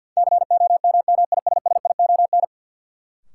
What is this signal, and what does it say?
Morse code, heard as a keyed tone, about 53 wpm, 700 Hz. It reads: COMMISSION
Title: Sentences from Top 1400 Words - New Word Drill - 50wpm